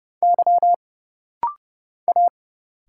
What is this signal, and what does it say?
Morse code, heard as a keyed tone, about 30 wpm, 700 Hz. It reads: Y A
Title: Single Letters - Rapid-Fire 30wpm